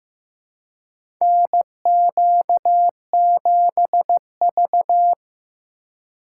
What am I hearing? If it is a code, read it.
Morse code, heard as a keyed tone, about 15 wpm, 700 Hz. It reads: NQ7V
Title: USA Call Signs - 4 Characters - 3x Character Spacing - 15wpm